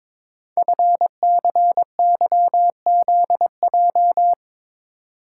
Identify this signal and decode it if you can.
Morse code, heard as a keyed tone, about 22 wpm, 700 Hz. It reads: FCYZJ
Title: Code Groups - 5 Letters - 22wpm